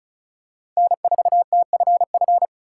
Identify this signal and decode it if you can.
Morse code, heard as a keyed tone, about 36 wpm, 700 Hz. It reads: N4TFF
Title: USA Call Signs - 3x Character Spacing - 35wpm